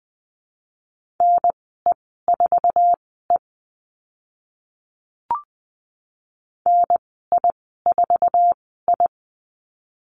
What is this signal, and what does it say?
Morse code, heard as a keyed tone, about 20 wpm, spaced as for 14 wpm, 700 Hz. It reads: NE4E NI4I
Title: 082 Lesson 14g - Number 2 - Call Signs 20wpm